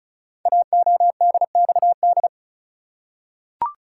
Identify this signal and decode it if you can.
Morse code, heard as a keyed tone, about 36 wpm, 690 Hz. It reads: AODXD
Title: Code Groups - 5 Letters - 35wpm